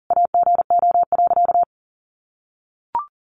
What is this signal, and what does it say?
Morse code, heard as a keyed tone, about 40 wpm, 700 Hz. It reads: AGO.
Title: Sentences from Top 500 Words 40wpm